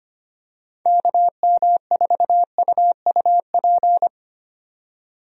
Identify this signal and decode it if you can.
Morse code, heard as a keyed tone, about 25 wpm, 700 Hz. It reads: KM4UUP
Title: USA Call Signs - 6 Characters - 25wpm